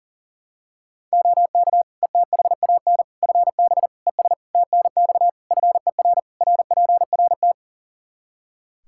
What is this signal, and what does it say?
Morse code, heard as a keyed tone, about 40 wpm, 700 Hz. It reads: OK ETHAN FB ES TNX FER RPRT
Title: Practice QSO - Exchange 2 - Info - 40wpm